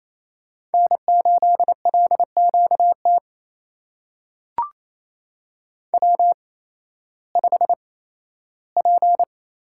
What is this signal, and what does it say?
Morse code, heard as a keyed tone, about 28 wpm, spaced as for 15 wpm, 700 Hz. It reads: N8LQT W5P
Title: USA Call Signs - 8x Character Spacing - 28wpm